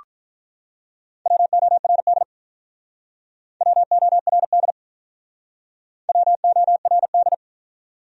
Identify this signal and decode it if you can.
Morse code, heard as a keyed tone, about 48 wpm, 700 Hz. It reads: WORD WORD WORD